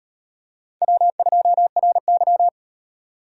Morse code, heard as a keyed tone, about 37 wpm, 700 Hz. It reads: W2RY